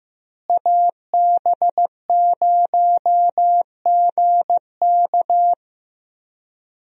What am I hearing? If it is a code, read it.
Morse code, heard as a keyed tone, about 15 wpm, 700 Hz. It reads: AB0GK
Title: USA Call Signs - 5 Characters - Repeated 2x - 15wpm